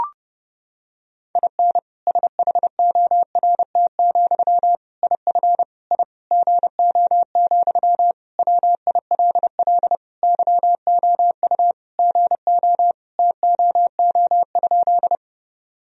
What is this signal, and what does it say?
Morse code, heard as a keyed tone, about 30 wpm, 700 Hz. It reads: IN SHORT, IF I GO, WILL YOU GO TOO?
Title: Sentences from Top 1000 Words - Review Part 2 - 30wpm